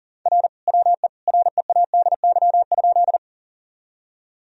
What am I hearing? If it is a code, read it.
Morse code, heard as a keyed tone, about 40 wpm, 700 Hz. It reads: R WE READY?
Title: Sentences from Top 400 Words and Abbreviations 40wpm